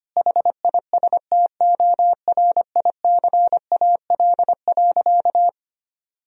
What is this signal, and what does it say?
Morse code, heard as a keyed tone, about 25 wpm, 700 Hz. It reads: HISTORICAL.
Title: Sentences from Top 1000 Words - Review Part 4 - 25wpm